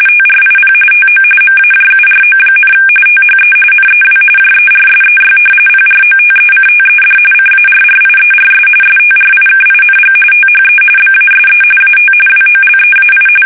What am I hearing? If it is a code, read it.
RTTY, about 45.45 baud. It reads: TRIP TO MAJURO, THE NEW JERSEY PROTECTED CARRIERS STRIKING THE BASE AT WOLEAI
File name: fsk850 / battleshipnj